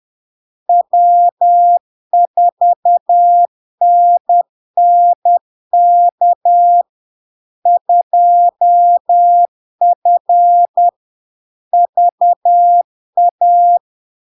Morse code, heard as a keyed tone, about 10 wpm, 700 Hz. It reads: W4NNK 2F VA
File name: LARK-FD-2022-Log-in-CW-10wpm-700-Hz